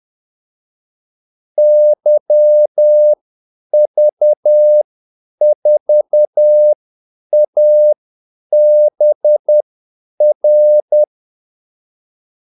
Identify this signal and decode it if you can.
Morse code, heard as a keyed tone, about 10 wpm, spaced as for 8 wpm, 600 Hz. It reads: YV4ABR
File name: Callsigns-10-8wpm